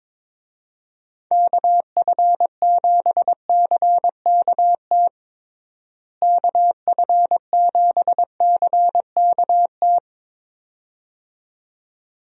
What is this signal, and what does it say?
Morse code, heard as a keyed tone, about 22 wpm, 700 Hz. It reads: KF7CKT KF7CKT
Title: USA Call Signs - 6 Characters - Repeated 2x - 22wpm